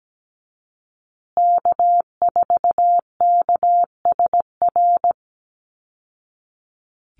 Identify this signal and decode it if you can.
Morse code, heard as a keyed tone, about 17 wpm, 700 Hz. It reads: K4KSR
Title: Call Signs Repeated 2x - 17wpm